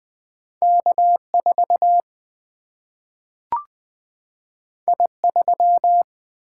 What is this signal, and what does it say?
Morse code, heard as a keyed tone, about 20 wpm, 700 Hz. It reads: K 4 I 3
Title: Single-Letter Single-Number 20wpm